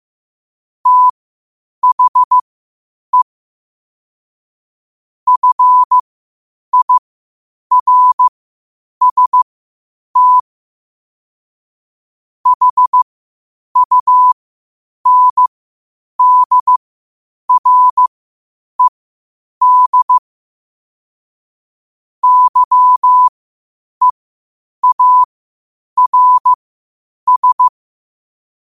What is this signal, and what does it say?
Morse code, heard as a keyed tone, about 15 wpm, spaced as for 8 wpm, 1000 Hz. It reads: THE FIRST HUNDRED YEARS
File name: qotdfiles / 8wpm